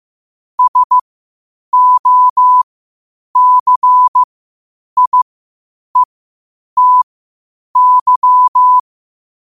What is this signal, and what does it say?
Morse code, heard as a keyed tone, about 15 wpm, spaced as for 8 wpm, 1000 Hz. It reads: SOCIETY
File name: qotdfiles / 8wpm